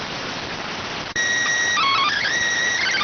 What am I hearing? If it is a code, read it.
SSTV (slow-scan television), PD 290.